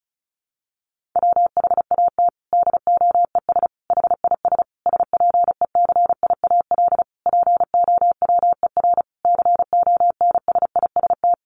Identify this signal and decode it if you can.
Morse code, heard as a keyed tone, about 34 wpm, 700 Hz. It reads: WHAT DOES HIS SPECIAL POWER CONSIST
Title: Sentences from Top 1400 Words 35wpm